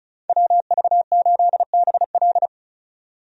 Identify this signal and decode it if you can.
Morse code, heard as a keyed tone, about 36 wpm, 700 Hz. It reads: WV8BL
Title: USA Call Signs - 5 Characters - 35wpm